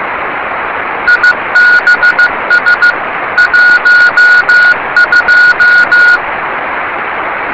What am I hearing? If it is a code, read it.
Morse code, heard as a keyed tone, about 15 wpm, 1450 Hz. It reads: IBS 12
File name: yubileiry cw by dk3wn